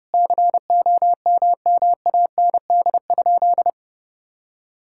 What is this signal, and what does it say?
Morse code, heard as a keyed tone, about 30 wpm, 700 Hz. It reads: COMMAND?
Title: Sentences from Top 1600 Words 30wpm